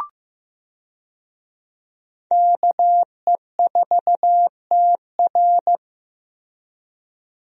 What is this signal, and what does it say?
Morse code, heard as a keyed tone, about 15 wpm, 700 Hz. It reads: KE4TR
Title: USA Call Signs - 5 Characters - Repeated 3x - 15wpm